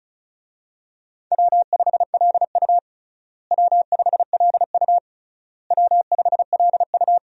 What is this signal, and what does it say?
Morse code, heard as a keyed tone, about 36 wpm, 700 Hz. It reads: W5LU W5LU W5LU
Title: USA Call Signs - 4 Characters - Repeated 3x - 35wpm